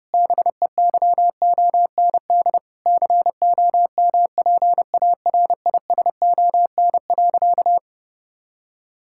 Morse code, heard as a keyed tone, about 32 wpm, 700 Hz. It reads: BEYOND COMPARISON.
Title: Sentences from Top 1600 Words 30wpm